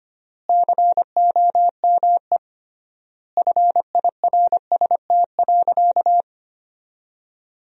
Morse code, heard as a keyed tone, about 25 wpm, 700 Hz. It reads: COME FIRST.
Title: 3x Word-Spacing - Sentences from Top 300 Words - 25wpm